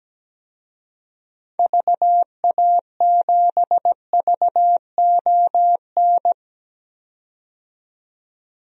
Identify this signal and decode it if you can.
Morse code, heard as a keyed tone, about 17 wpm, 700 Hz. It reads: VA7VON
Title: DXCC-100 Call Signs - 6 Characters - 17wpm